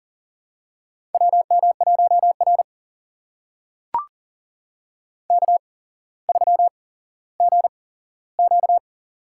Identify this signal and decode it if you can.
Morse code, heard as a keyed tone, about 39 wpm, spaced as for 17 wpm, 700 Hz. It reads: WM1R K3GQ
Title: USA Call Signs - 4 Characters - 8x Character Spacing - 40wpm